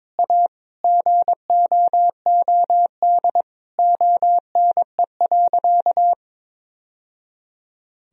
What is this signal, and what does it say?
Morse code, heard as a keyed tone, about 22 wpm, 700 Hz. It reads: A GOOD ONE.
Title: Sentences from Top 1000 Words - Review Part 3 - 22wpm